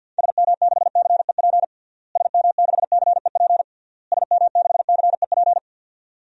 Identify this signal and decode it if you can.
Morse code, heard as a keyed tone, about 52 wpm, 700 Hz. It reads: SM6CEP SM6CEP SM6CEP
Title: DXCC-100 Call Signs - 6 Characters - Repeated 3x - 50wpm